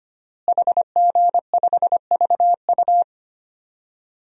Morse code, heard as a keyed tone, about 25 wpm, 700 Hz. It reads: HG5VU
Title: DXCC-100 Call Signs - 5 Characters - Repeated 3x - 25wpm